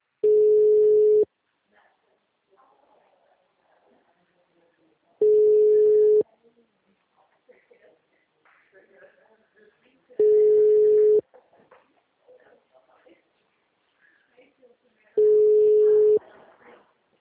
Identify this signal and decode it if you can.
Ringback tone (Europe, 425 Hz), with speech.